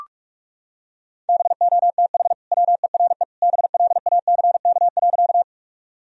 Morse code, heard as a keyed tone, about 44 wpm, 700 Hz. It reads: BOTH WERE BLACK.
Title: Sentences from Top 1000 Words - Review Part 2 - 45wpm